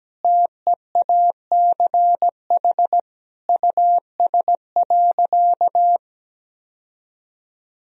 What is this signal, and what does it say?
Morse code, heard as a keyed tone, about 17 wpm, 700 Hz. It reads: TEACH US.
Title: Sentences from Top 1000 Words - Review Part 2 - 17wpm